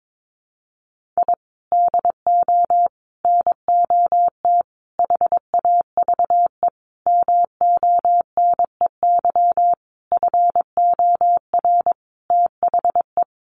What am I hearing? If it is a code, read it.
Morse code, heard as a keyed tone, about 22 wpm, 700 Hz. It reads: I DO NOT HAVE MONEY FOR THE
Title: Sentences from Top 1400 Words 22wpm